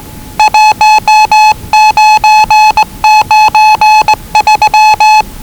Morse code, heard as a keyed tone, about 20 wpm, 840 Hz. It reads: 1993
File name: WellingtonRadioZlwCloseDownOnHfAndMf / 01-Zlw 4277khz 417.5 qrt until 0925z